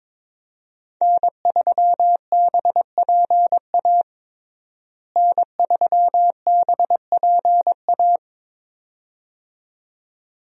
Morse code, heard as a keyed tone, about 22 wpm, 700 Hz. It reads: N3BPA N3BPA
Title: USA Call Signs - 5 Characters - Repeated 2x - 22wpm